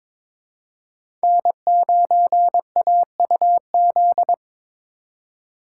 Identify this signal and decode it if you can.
Morse code, heard as a keyed tone, about 22 wpm, 700 Hz. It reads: N9AUZ
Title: USA Call Signs - 5 Characters - 4x Character Spacing - 22wpm